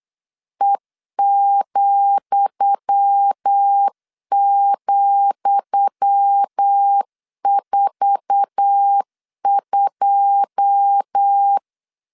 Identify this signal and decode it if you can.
Morse code, heard as a keyed tone, about 8 wpm, 800 Hz. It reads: E,,42
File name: LEZIONI / 10 SIGNS / 10D COMA 8wpm 3749